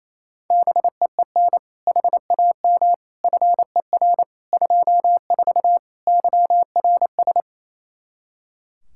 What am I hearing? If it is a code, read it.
Morse code, heard as a keyed tone, about 28 wpm, 700 Hz. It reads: BEEN HAM FER 24 YRS
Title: Practice QSO - Exchange 3 - More Info - 28wpm